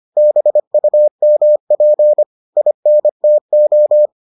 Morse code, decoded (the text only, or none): BUMP INTO